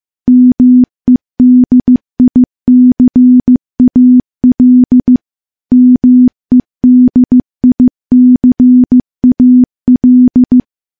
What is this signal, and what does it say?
Morse code, heard as a keyed tone, about 15 wpm, 250 Hz. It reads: MEDICAL MEDICAL
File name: morse / morse-training-15WPM